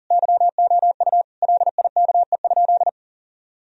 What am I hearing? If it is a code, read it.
Morse code, heard as a keyed tone, about 37 wpm, 700 Hz. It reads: YOU LIKE?
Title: Sentences from Top 1000 Words - Review Part 3 - 40wpm